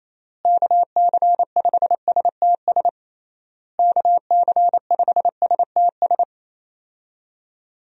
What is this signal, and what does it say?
Morse code, heard as a keyed tone, about 28 wpm, 700 Hz. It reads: KC5STS KC5STS
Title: USA Call Signs - 6 Characters - Repeated 3x - 28wpm